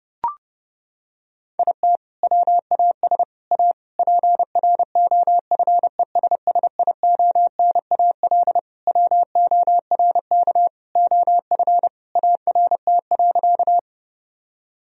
Morse code, heard as a keyed tone, about 30 wpm, 700 Hz. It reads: IT WAS A PROFESSIONAL WORK OF ART.
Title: Sentences from Top 1000 Words - Review Part 4 - 30wpm